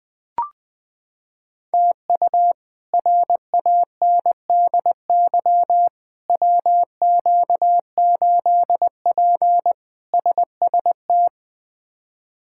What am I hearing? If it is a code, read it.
Morse code, heard as a keyed tone, about 20 wpm, 700 Hz. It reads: TU RANDY WQ8P SST
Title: K1USN SST QSO Practice - 20wpm